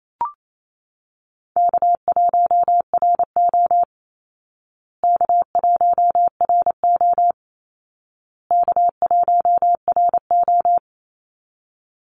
Morse code, heard as a keyed tone, about 28 wpm, 700 Hz. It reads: K1RO K1RO K1RO
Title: Easy Call Signs Repeated 3x - 28wpm